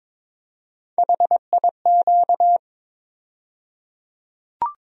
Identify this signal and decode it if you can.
Morse code, heard as a keyed tone, about 22 wpm, 700 Hz. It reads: HIQ
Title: USA Call Sign Postfixes - 3 Letters 22wpm